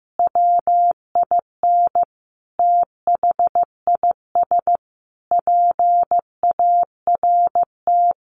Morse code, heard as a keyed tone, about 15 wpm, 700 Hz. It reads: WIN THIS PART
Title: Sentences from Top 1500 Words 15wpm